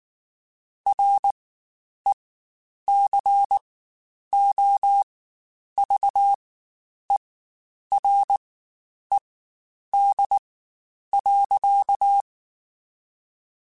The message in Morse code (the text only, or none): RECOVERED.